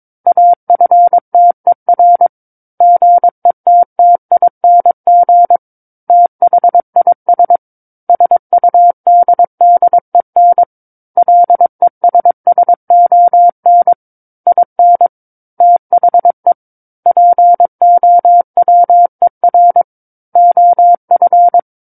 Morse code, heard as a keyed tone, about 22 wpm, 700 Hz. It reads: AFTER GETTING THIS SUDDEN LESSON IN THE POWER OF
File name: Book One / 13-Chapter 13 - 22 WPM